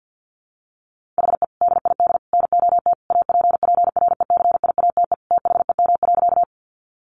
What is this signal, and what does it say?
Morse code, heard as a keyed tone, about 50 wpm, 700 Hz. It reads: HE DID NOT APPRECIATE THEM.
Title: Sentences from Top 1300 Words 50wpm